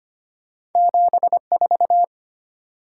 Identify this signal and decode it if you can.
Morse code, heard as a keyed tone, about 25 wpm, 700 Hz. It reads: 74
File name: Two Letter-Number Combinations - 25wpm